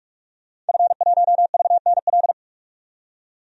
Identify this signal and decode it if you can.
Morse code, heard as a keyed tone, about 46 wpm, 700 Hz. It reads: F1VNL
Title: DXCC-100 Call Signs - 5 Characters - 45wpm